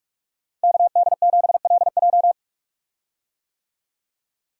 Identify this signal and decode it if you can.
Morse code, heard as a keyed tone, about 41 wpm, 700 Hz. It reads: KD7LJ
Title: USA Call Signs - 5 Characters - Repeated 3x - 45wpm